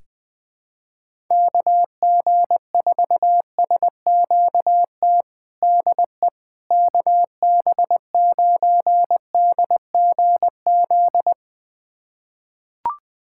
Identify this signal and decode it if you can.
Morse code, heard as a keyed tone, about 21 wpm, 700 Hz. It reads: KG4SQT DE KB9DGZ
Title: USA Callsign DE Callsign - 6 Characters - 20wpm